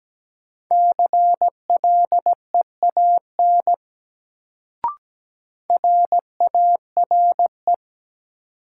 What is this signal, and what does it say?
Morse code, heard as a keyed tone, about 17 wpm, 700 Hz. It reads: CLEAN RARE
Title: Sentences from Top 1100 Words - New Word Drill - 17wpm